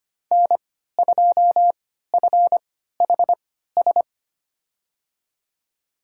Morse code, heard as a keyed tone, about 25 wpm, spaced as for 14 wpm, 700 Hz. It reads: N2FHS
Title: USA Call Signs - 3x Character Spacing - 25wpm